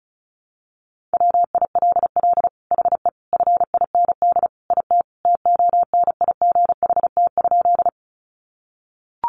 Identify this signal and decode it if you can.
Morse code, heard as a keyed tone, about 34 wpm, 700 Hz. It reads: WILL HE FIND IT TONIGHT?
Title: Sentences from Top 600 Words 35wpm